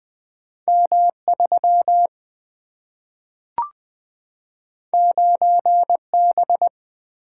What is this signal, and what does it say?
Morse code, heard as a keyed tone, about 20 wpm, 700 Hz. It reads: M 3 9 B
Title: Single-Letter Single-Number 20wpm